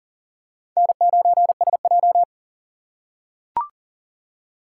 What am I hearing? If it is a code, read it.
Morse code, heard as a keyed tone, about 40 wpm, 700 Hz. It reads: N9SJ
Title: USA Call Signs - 4 Characters - 2x Character Spacing - 40wpm